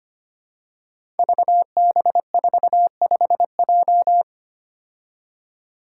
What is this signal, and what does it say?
Morse code, heard as a keyed tone, about 25 wpm, 700 Hz. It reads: VB45J
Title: Code Groups - 5 Letter:Numbers - 25wpm